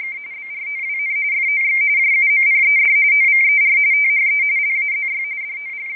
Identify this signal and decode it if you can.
RTTY, about 45.45 baud. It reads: OWN FOX JUMPED OVER THE LAZY
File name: rtty / websdr recording 14086.0kHz